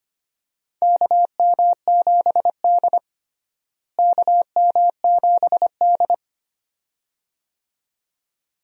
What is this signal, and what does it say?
Morse code, heard as a keyed tone, about 25 wpm, 700 Hz. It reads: KM7D KM7D
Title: USA Call Signs - 4 Characters - Repeated 2x - 25wpm